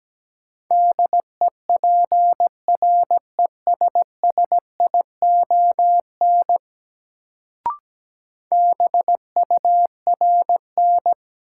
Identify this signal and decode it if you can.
Morse code, heard as a keyed tone, about 17 wpm, 700 Hz. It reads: DEPRESSION BURN
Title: Sentences from Top 1300 Words - New Word Drill - 17wpm